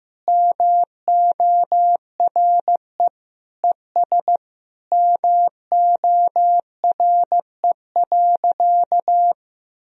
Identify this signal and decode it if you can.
Morse code, heard as a keyed tone, about 16 wpm, 700 Hz. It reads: MORE ES MORE.
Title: Sentences from Top 500 Words and Abbreviations 15wpm